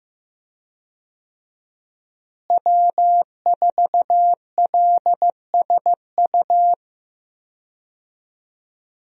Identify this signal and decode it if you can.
Morse code, heard as a keyed tone, about 15 wpm, 700 Hz. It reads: W4LSU
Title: USA Call Signs - 5 Characters - Repeated 3x - 15wpm